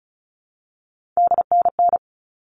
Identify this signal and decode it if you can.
Morse code, heard as a keyed tone, about 34 wpm, 700 Hz. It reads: DNN
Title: All Cut Numbers - 3-Digit 35wpm